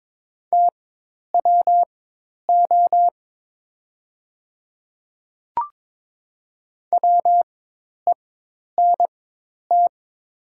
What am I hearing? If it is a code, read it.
Morse code, heard as a keyed tone, about 22 wpm, spaced as for 10 wpm, 700 Hz. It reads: TWO WENT
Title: 4x Character Spacing - Top 300 Words - 22wpm